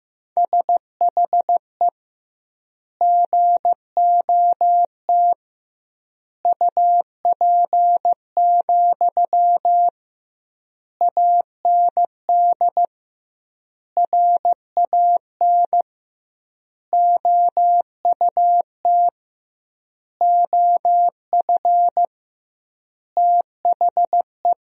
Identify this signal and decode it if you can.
Morse code, heard as a keyed tone, about 15 wpm, 700 Hz. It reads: SHE GOT UP, AND RAN OUT OF THE
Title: 2x Word-Spacing - Sentences from Top 500 Words - 15wpm